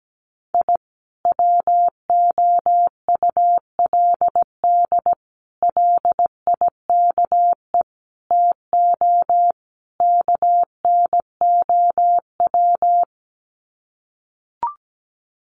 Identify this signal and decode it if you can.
Morse code, heard as a keyed tone, about 17 wpm, 700 Hz. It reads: I WOULD LIKE TO KNOW
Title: Sets of 5 Words Encore 17wpm